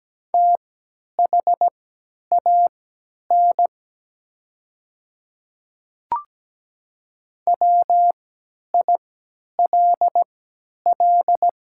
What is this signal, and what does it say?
Morse code, heard as a keyed tone, about 17 wpm, spaced as for 10 wpm, 700 Hz. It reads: THAN WILL
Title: 3x Character Spacing - Top 100 Words - 17wpm